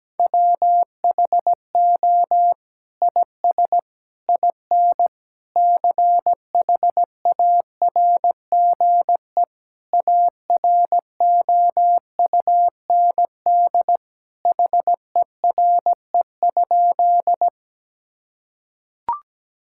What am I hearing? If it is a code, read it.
Morse code, heard as a keyed tone, about 17 wpm, 700 Hz. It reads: WHO IS IN CHARGE AROUND HERE?
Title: Sentences from Top 1000 Words - Review Part 6 - 17wpm